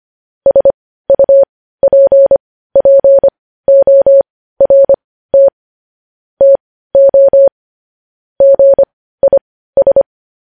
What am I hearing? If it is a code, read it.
Morse code, heard as a keyed tone, about 25 wpm, spaced as for 15 wpm, 550 Hz. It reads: SUPPORT TO GIS
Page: news headlines in morse code at 15 wpm